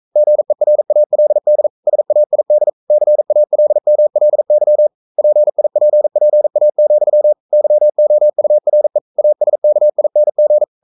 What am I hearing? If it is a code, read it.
Morse code, heard as a keyed tone, about 42 wpm, 600 Hz. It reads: GERALD SAID CALMLY PIPPA, YOURE ASKING
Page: Romanen Never af Ken Follet på engelsk.